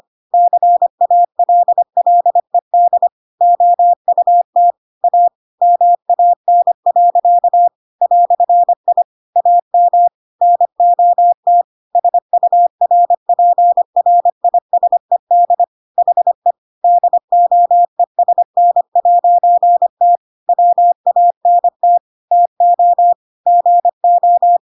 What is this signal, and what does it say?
Morse code, heard as a keyed tone, about 25 wpm, 700 Hz. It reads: CALLED OUT A MAN. "I AM NOT SURPRISED HE DOESN'T WANT TO GO
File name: Pinocchio / Pinocchio ch3 25 25 ik1hge